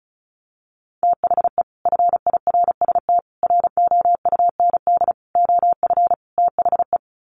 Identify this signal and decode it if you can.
Morse code, heard as a keyed tone, about 34 wpm, 700 Hz. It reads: THE FIRST ROUND OF THE
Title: Sets of 5 Words Encore 35wpm